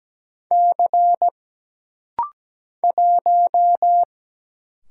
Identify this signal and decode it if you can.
Morse code, heard as a keyed tone, about 17 wpm, 700 Hz. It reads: C 1
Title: Single Letter-Number - Rapid-Fire 17wpm